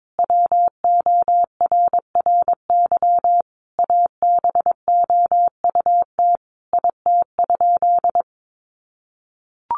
Morse code, heard as a keyed tone, about 22 wpm, 700 Hz. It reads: WORRY ABOUT IT?
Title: Sentences from Top 1100 Words 22wpm